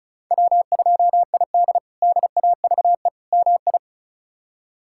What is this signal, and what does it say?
Morse code, heard as a keyed tone, about 36 wpm, 700 Hz. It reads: W2ID DAVE MI
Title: K1USN SST QSO Practice - 35wpm